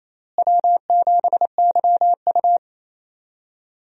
Morse code, heard as a keyed tone, about 28 wpm, 700 Hz. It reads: W7YU
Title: USA Call Signs - 4 Characters - Repeated 3x - 28wpm